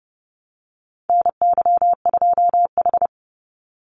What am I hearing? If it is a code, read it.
Morse code, heard as a keyed tone, about 30 wpm, 700 Hz. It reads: NY2H
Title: Call Signs Repeated 2x - 30wpm